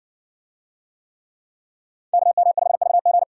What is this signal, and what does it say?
Morse code, heard as a keyed tone, about 60 wpm, 700 Hz. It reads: KM5VG